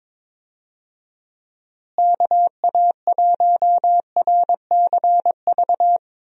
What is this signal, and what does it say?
Morse code, heard as a keyed tone, about 22 wpm, 700 Hz. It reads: KA1RCV
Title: USA Call Signs - 6 Characters - Repeated 2x - 22wpm